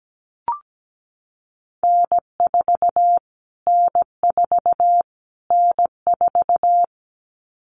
Morse code, heard as a keyed tone, about 17 wpm, 700 Hz. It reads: N4 N4 N4
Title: Callsign Prefixes for Top 25 DXCC - Repeated 3x - 17wpm